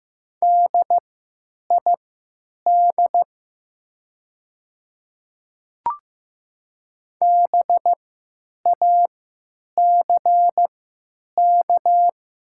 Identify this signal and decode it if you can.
Morse code, heard as a keyed tone, about 15 wpm, spaced as for 9 wpm, 700 Hz. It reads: DID BACK
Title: 3x Character Spacing - Top 200 Words - 15wpm